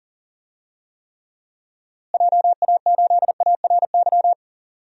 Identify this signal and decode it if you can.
Morse code, heard as a keyed tone, about 40 wpm, 700 Hz. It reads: JA8ARY
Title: DXCC-100 Call Signs - 6 Characters - Repeated 2x - 40wpm